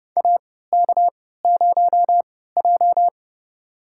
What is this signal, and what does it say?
Morse code, heard as a keyed tone, about 30 wpm, spaced as for 17 wpm, 700 Hz. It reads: AK0J
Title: USA Call Signs - 4 Characters - 3x Character Spacing - 30wpm